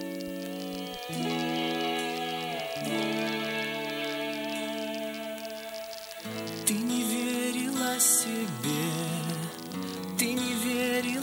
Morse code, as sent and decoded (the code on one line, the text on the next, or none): none